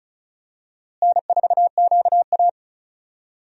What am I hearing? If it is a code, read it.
Morse code, heard as a keyed tone, about 36 wpm, 700 Hz. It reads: N4QA
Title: USA Call Signs - 4 Characters - Repeated 3x - 35wpm